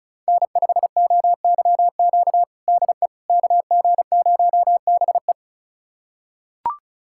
Morse code, heard as a keyed tone, about 36 wpm, 700 Hz. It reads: N5OYQ DE KG0BE
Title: USA Callsign DE Callsign 35wpm